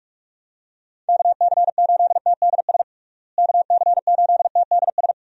Morse code, heard as a keyed tone, about 41 wpm, 700 Hz. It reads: KC8TDS KC8TDS
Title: USA Call Signs - 6 Characters - Repeated 3x - 45wpm